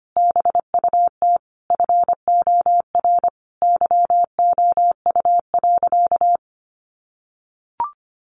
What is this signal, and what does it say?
Morse code, heard as a keyed tone, about 25 wpm, 700 Hz. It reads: BUT FOR YOU.
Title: Sentences from Top 1300 Words 25wpm